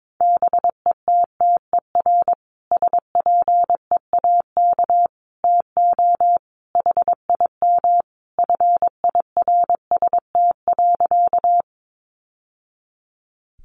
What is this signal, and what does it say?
Morse code, heard as a keyed tone, about 22 wpm, 700 Hz. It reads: BETTER SPEAK TO HIM FIRST.
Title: Sentences from Top 1100 Words 22wpm